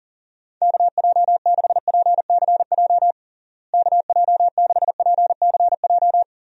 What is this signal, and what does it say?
Morse code, heard as a keyed tone, about 40 wpm, 700 Hz. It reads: KJ6PCJ KJ6PCJ
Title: USA Call Signs - 6 Characters - Repeated 2x - 40wpm